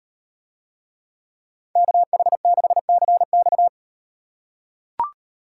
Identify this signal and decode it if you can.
Morse code, heard as a keyed tone, about 38 wpm, 700 Hz. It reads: KH6CX